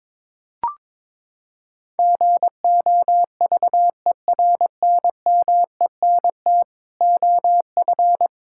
Morse code, heard as a keyed tone, about 22 wpm, 700 Hz. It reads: GOVERNMENT OF
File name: Sets of 9 Words - 22wpm